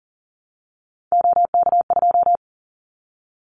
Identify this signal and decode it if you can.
Morse code, heard as a keyed tone, about 40 wpm, 700 Hz. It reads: OK2
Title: Callsign Prefixes for Top 100 DXCC - 40wpm